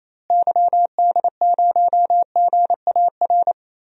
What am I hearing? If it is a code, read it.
Morse code, heard as a keyed tone, about 28 wpm, 700 Hz. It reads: YD0GAR
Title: DXCC-100 Call Signs - 6 Characters - Repeated 2x - 28wpm